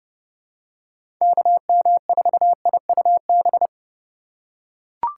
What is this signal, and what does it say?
Morse code, heard as a keyed tone, about 30 wpm, 700 Hz. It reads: KM4IUB
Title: USA Call Signs - 6 Characters - 4x Character Spacing - 30wpm